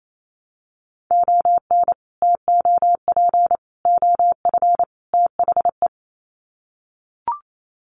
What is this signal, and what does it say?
Morse code, heard as a keyed tone, about 28 wpm, 700 Hz. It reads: ON TOP OF THE
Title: Sets of 4 Words 28wpm